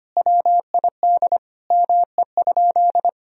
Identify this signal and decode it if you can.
Morse code, heard as a keyed tone, about 25 wpm, 700 Hz. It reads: WID ME?
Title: Sentences from Top 200 Words and Abbreviations 25wpm